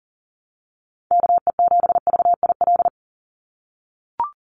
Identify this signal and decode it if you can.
Morse code, heard as a keyed tone, about 40 wpm, 700 Hz. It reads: KE7VIL
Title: USA Call Signs 40wpm